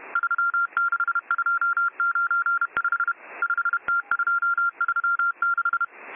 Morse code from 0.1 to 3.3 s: 36295